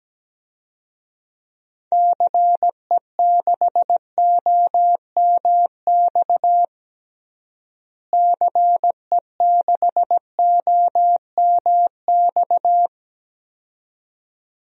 Morse code, heard as a keyed tone, about 17 wpm, 700 Hz. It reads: CE6OMX CE6OMX
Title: DXCC-100 Call Signs - 6 Characters - Repeated 2x - 17wpm